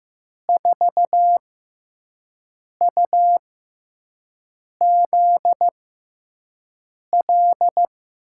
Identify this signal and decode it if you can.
Morse code, heard as a keyed tone, about 15 wpm, spaced as for 5 wpm, 700 Hz. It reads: 4UZL
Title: USA Call Signs - 6 Characters - 6x Character Spacing - 15wpm